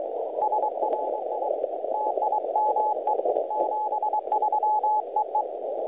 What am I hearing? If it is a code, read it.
Morse code, heard as a keyed tone, about 22 wpm, 850 Hz, with noise. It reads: SRI TIME 73 EE
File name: truSDX / websdr recording 3541.9kHz